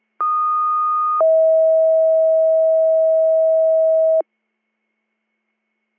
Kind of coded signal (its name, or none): two-tone page